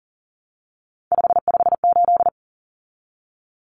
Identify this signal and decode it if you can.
Morse code, heard as a keyed tone, about 40 wpm, 700 Hz. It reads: 558
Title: Common Cut Numbers - 3-Digit 40wpm